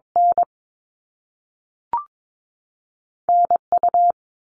Morse code, heard as a keyed tone, about 22 wpm, 700 Hz. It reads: N NU
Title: Canadian Provinces and Territories 22wpm